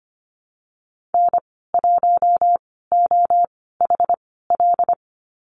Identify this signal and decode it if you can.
Morse code, heard as a keyed tone, about 25 wpm, spaced as for 16 wpm, 700 Hz. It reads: N1OHL
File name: Lesson 15g - Number 5 - Call Signs 25wpm